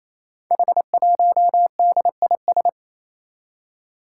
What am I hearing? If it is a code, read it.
Morse code, heard as a keyed tone, about 29 wpm, 700 Hz. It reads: H1DIS
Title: Five Challenging Letter-Number Combinations 28wpm